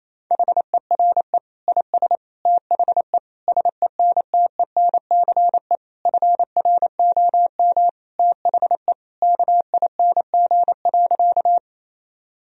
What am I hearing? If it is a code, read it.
Morse code, heard as a keyed tone, about 28 wpm, 700 Hz. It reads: HERE IS THE SENTENCE FROM THE KING.
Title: Sentences from Top 1000 Words - Review Part 1 - 28wpm